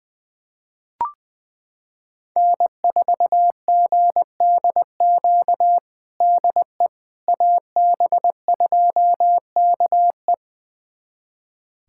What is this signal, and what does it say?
Morse code, heard as a keyed tone, about 20 wpm, 700 Hz. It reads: N4GDQ DE AB2KE
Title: USA Callsign DE Callsign - 5 Characters - 20wpm